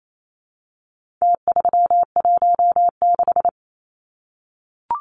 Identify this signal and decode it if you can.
Morse code, heard as a keyed tone, about 28 wpm, 700 Hz. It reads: T316
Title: Numbers - Contest Style - 4-Digit 28wpm